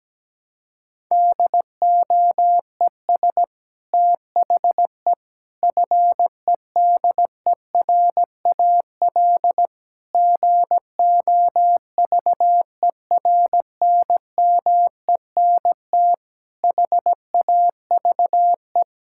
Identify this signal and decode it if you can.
Morse code, heard as a keyed tone, about 17 wpm, 700 Hz. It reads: DOES THE FEDERAL GOVERNMENT HAVE
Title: Sentences from Top 1000 Words - Review Part 5 - 17wpm